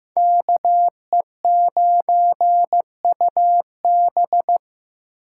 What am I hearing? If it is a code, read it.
Morse code, heard as a keyed tone, about 15 wpm, 700 Hz. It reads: KE9UB
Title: USA Call Signs - 5 Characters - 8x Character Spacing - 15wpm